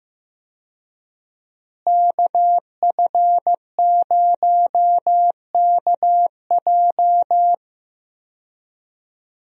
Morse code, heard as a keyed tone, about 15 wpm, 700 Hz. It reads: KF0KJ
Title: USA Call Signs - 5 Characters - Repeated 2x - 15wpm